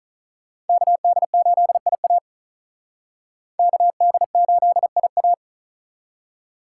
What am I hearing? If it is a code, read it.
Morse code, heard as a keyed tone, about 45 wpm, 700 Hz. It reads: KD8IA KD8IA